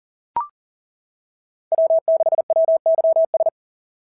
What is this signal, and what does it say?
Morse code, heard as a keyed tone, about 40 wpm, 650 Hz. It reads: W6WYS
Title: SOTA QSO Practice 40wpm